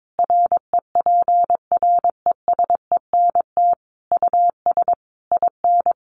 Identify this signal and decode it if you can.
Morse code, heard as a keyed tone, about 22 wpm, 700 Hz. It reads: REPRESENT US IN
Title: Sentences from Top 1200 Words 22wpm